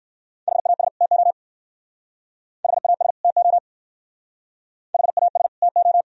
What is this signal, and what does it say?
Morse code, heard as a keyed tone, about 55 wpm, 700 Hz. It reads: HAS TO HAS TO HAS TO
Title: Speed-Racing - Sets of 2 Words 45wpm